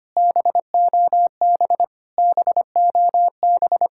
Morse code, heard as a keyed tone, about 30 wpm, 700 Hz. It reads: BOB BOB
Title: Speed-Racing Common QSO Phrases Encore - 25wpm